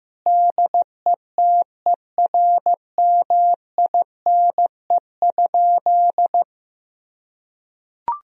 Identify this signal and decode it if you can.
Morse code, heard as a keyed tone, about 15 wpm, 700 Hz. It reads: DETERMINE?
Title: Sentences from Top 1000 Words - Review Part 4 - 15wpm